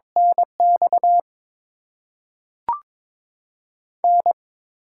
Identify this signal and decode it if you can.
Morse code, heard as a keyed tone, about 22 wpm, 700 Hz. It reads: NX N
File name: Top 20 QSO Elements - 8x Character Spacing - 22wpm